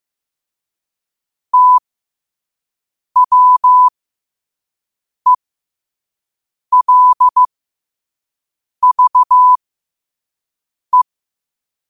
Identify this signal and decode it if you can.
Morse code, heard as a keyed tone, about 15 wpm, spaced as for 5 wpm, 1000 Hz. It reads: TWELVE
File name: qotdfiles / 5wpm